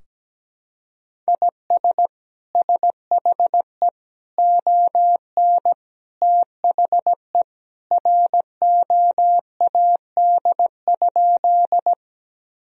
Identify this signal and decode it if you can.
Morse code, heard as a keyed tone, about 17 wpm, 700 Hz. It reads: IS SHE ON THE ROAD?
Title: Sentences from Top 1000 Words - Review Part 2 - 17wpm